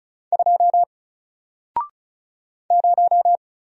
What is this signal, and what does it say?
Morse code, heard as a keyed tone, about 35 wpm, 700 Hz. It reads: 2 0
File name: Single-Letter-Number-35wpm